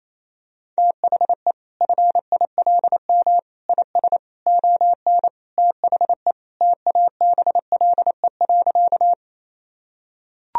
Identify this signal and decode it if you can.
Morse code, heard as a keyed tone, about 27 wpm, 700 Hz. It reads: THE FILM IS ON THE TABLE.
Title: Sentences from Top 1000 Words - Review Part 3 - 28wpm